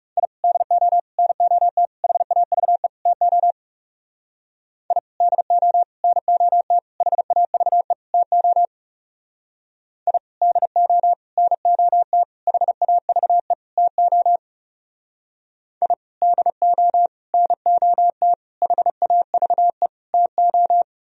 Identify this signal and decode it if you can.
Morse code, heard as a keyed tone, about 38 wpm, 700 Hz. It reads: I DO NOT HAVE TO I DO NOT HAVE TO I DO NOT HAVE TO I DO NOT HAVE TO